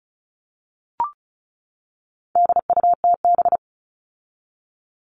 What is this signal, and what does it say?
Morse code, heard as a keyed tone, about 34 wpm, 700 Hz. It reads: DUTB
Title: All Cut Numbers - 4-Digit 35wpm